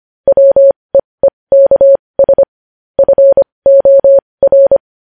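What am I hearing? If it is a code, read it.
Morse code, heard as a keyed tone, about 25 wpm, spaced as for 20 wpm, 550 Hz. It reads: WEEKS FOR
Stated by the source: Morse code news headlines practice at 20 wpm.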